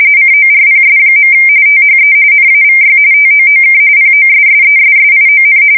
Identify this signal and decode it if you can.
RTTY, about 45.45 baud. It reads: CORPS. THE NAVY TOOK POSSESSION
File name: fsk170 / usnavy4